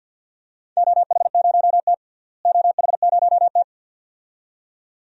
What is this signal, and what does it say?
Morse code, heard as a keyed tone, about 53 wpm, 700 Hz. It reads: OH0T OH0T
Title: DXCC-100 Call Signs - 4 Characters - Repeated 2x - 50wpm